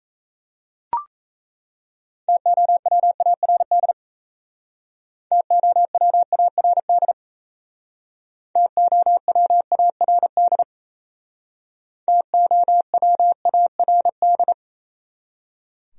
Morse code, heard as a keyed tone, about 36 wpm, 700 Hz. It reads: TOWARD TOWARD TOWARD TOWARD